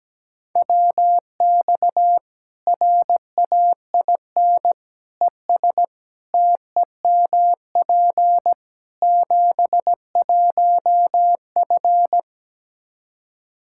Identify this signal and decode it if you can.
Morse code, heard as a keyed tone, about 17 wpm, 700 Hz. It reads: WX RAIN ES TEMP 71F
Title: K1USN SST QSO Practice 17wpm